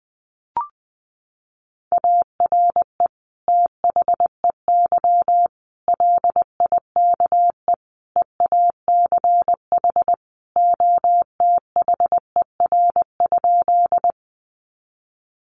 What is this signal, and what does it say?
Morse code, heard as a keyed tone, about 20 wpm, 700 Hz. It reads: ARE THEY LIKE EACH OTHER?
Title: Sentences from Top 200 Words 20wpm